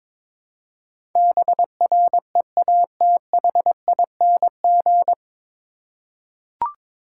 Morse code, heard as a keyed tone, about 21 wpm, 700 Hz. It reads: BREATHING
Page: Morse Code Ninja practice 22wpm